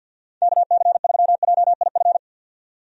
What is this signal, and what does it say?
Morse code, heard as a keyed tone, about 53 wpm, 700 Hz. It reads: KC3JIF
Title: USA Call Signs - 6 Characters - 50wpm